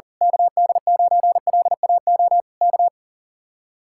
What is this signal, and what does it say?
Morse code, heard as a keyed tone, about 41 wpm, 700 Hz. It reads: KD9LAO K
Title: Speed-Racing Common QSO Phrases - 40wpm